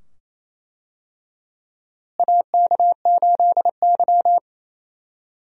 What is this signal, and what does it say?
Morse code, heard as a keyed tone, about 28 wpm, 700 Hz. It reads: AK8Y
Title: USA Call Signs - 4 Characters - Repeated 2x - 28wpm